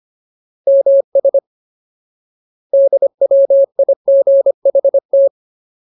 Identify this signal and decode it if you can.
Morse code, heard as a keyed tone, about 25 wpm, 550 Hz. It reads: MS DWIGHT
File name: Ultimate Word List / 25WPM Ultimate